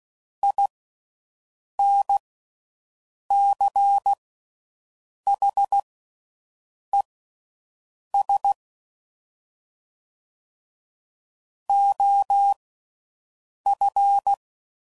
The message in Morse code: INCHES OF

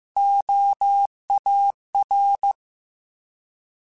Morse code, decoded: OAR

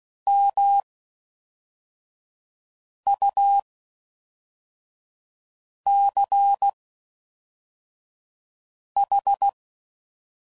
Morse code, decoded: MUCH

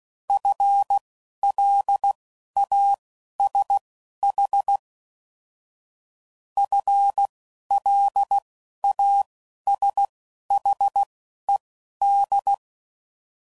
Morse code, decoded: FLASH FLASHED

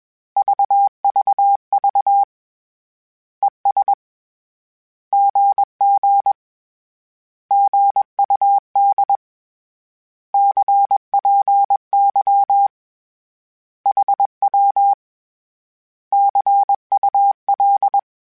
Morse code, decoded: VVV ES GG GUD CPY HW CUL

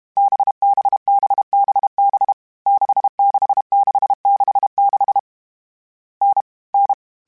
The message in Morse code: BBBBB 66666 N N

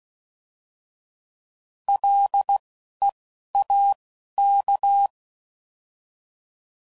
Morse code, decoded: LEAK

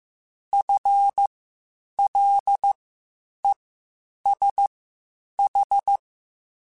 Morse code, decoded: FLESH

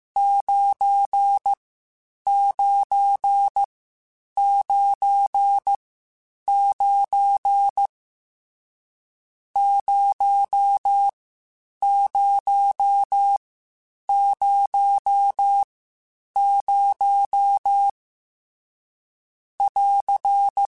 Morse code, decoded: 9999 0000 <AR>

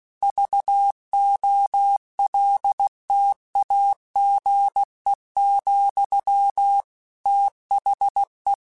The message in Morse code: VOLTAGE, THE

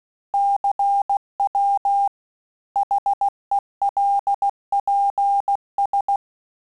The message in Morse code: CW HELPS

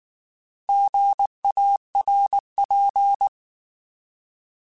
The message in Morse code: GARP

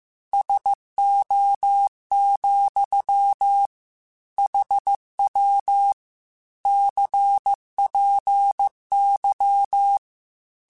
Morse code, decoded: SO, HW CPY

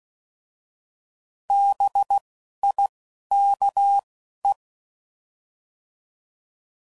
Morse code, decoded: B I K E

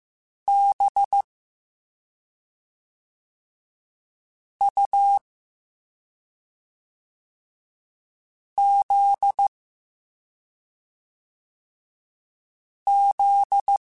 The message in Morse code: B U Z Z

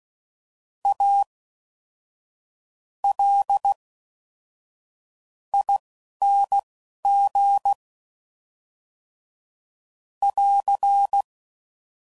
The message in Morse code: A L ING <AR>

